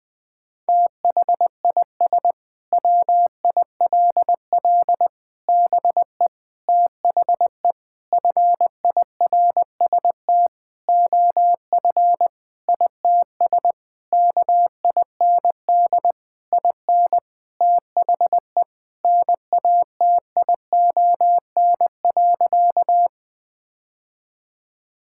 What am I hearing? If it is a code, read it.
Morse code, heard as a keyed tone, about 20 wpm, 700 Hz. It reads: THIS WILL BE THE FIRST OF ITS KIND IN THE NATION.
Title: Sentences with 8-Word Phrases - Spoken Beginning - 20wpm